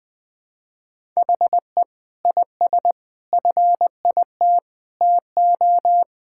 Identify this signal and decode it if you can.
Morse code, heard as a keyed tone, about 20 wpm, 700 Hz. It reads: HE IS FIT TO
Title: Sentences from Top 1000 Words - Review Part 5 - 20wpm